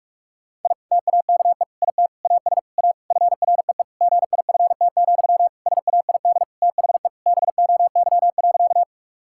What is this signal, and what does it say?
Morse code, heard as a keyed tone, about 44 wpm, 700 Hz. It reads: I TAKE IT AS A FREE GIFT, SAID THE BOY.
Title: Sentences from Top 1000 Words - Review Part 6 - 45wpm